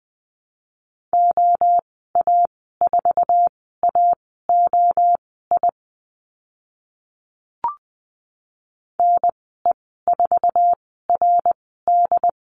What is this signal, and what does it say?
Morse code, heard as a keyed tone, about 20 wpm, spaced as for 14 wpm, 700 Hz. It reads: OA4AOI NE4RD
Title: 069 Lesson 12g - Letter d - Call Signs 20wpm